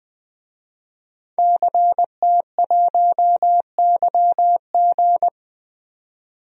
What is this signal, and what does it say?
Morse code, heard as a keyed tone, about 20 wpm, 700 Hz. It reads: CT1YG
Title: DXCC-100 Call Signs - 5 Characters - 20wpm